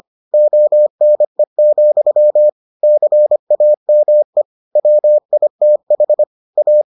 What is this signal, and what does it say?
Morse code, heard as a keyed tone, about 25 wpm, 600 Hz. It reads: ONE, CAME WITH A